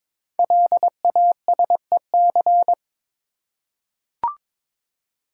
Morse code, heard as a keyed tone, about 22 wpm, 700 Hz. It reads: LASEC
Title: Code Groups - 5 Letters - 22wpm